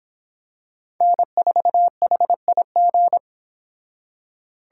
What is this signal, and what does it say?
Morse code, heard as a keyed tone, about 25 wpm, 700 Hz. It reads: N4HIG